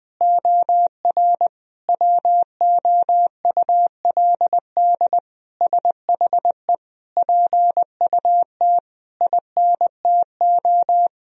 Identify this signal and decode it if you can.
Morse code, heard as a keyed tone, about 20 wpm, 700 Hz. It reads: OR WOULD SHE PUT INTO
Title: Sentences from Top 1000 Words - Review Part 1 - 20wpm